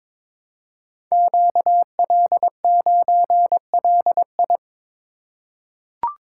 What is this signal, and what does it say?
Morse code, heard as a keyed tone, about 21 wpm, 700 Hz. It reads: QL9LI
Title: Code Groups - 5 Letter:Numbers - 22wpm